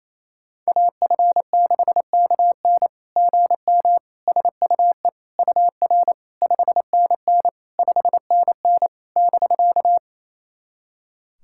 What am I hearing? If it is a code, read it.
Morse code, heard as a keyed tone, about 28 wpm, 700 Hz. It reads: AF6KN GM SUE UR 5NN 5NN <BK>
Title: SOTA QSO Practice 28wpm